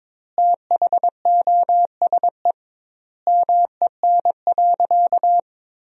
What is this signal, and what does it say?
Morse code, heard as a keyed tone, about 22 wpm, 700 Hz. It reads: THOSE MEN.
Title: 2x Word-Spacing - Sentences from Top 300 Words - 22wpm